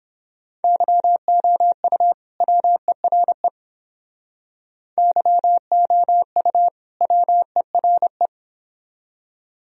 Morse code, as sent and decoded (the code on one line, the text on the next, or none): -.-- --- ..- / .-- . .-. / -.-- --- ..- / .-- . .-. .
YOU WERE YOU WERE